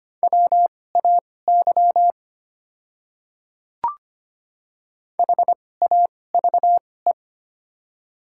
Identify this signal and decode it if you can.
Morse code, heard as a keyed tone, about 25 wpm, spaced as for 18 wpm, 690 Hz. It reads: WAY HAVE
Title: 2x Character Spacing - Top 100 Words - 25wpm